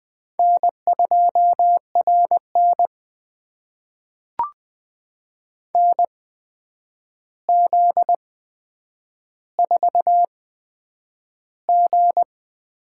Morse code, heard as a keyed tone, about 20 wpm, spaced as for 9 wpm, 700 Hz. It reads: N2RN NZ4G
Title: USA Call Signs - 4 Characters - 8x Character Spacing - 20wpm